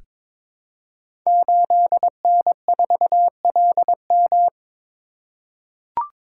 Morse code, heard as a keyed tone, about 22 wpm, 700 Hz. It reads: 8N4LM
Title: Code Groups - 5 Letter:Numbers - 22wpm